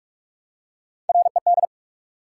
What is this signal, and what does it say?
Morse code, heard as a keyed tone, about 46 wpm, 700 Hz. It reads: RED
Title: Three Letter Words - 45wpm